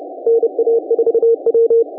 Morse code, heard as a keyed tone, about 31 wpm, 460 Hz. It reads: NA4W